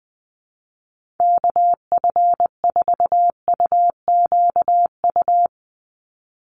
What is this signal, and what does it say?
Morse code, heard as a keyed tone, about 20 wpm, 700 Hz. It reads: KF4UQU
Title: USA Call Signs 20wpm